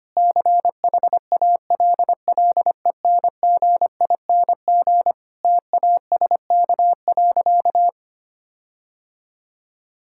Morse code, heard as a keyed tone, about 25 wpm, 700 Hz. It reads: CHALLENGING TASK.
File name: practice / 25wpm / Difficult Sentences with Hints - 25wpm